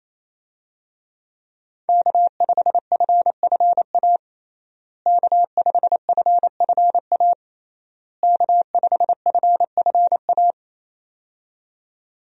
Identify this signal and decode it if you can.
Morse code, heard as a keyed tone, about 28 wpm, 700 Hz. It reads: K5FFA K5FFA K5FFA
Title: USA Call Signs - 5 Characters - Repeated 3x - 28wpm